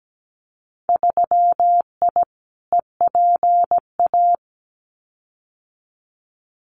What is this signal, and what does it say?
Morse code, heard as a keyed tone, about 17 wpm, 700 Hz. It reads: 3I EPA
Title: Winter Field Day Exchanges 17wpm